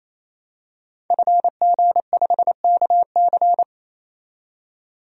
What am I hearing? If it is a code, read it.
Morse code, heard as a keyed tone, about 27 wpm, 700 Hz. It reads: FG5KC
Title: DXCC-100 Call Signs - 5 Characters - 28wpm